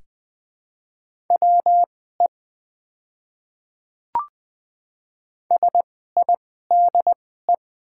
Morse code, heard as a keyed tone, about 20 wpm, spaced as for 14 wpm, 700 Hz. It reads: WE SIDE
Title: 2x Character Spacing - Top 200 Words - 20wpm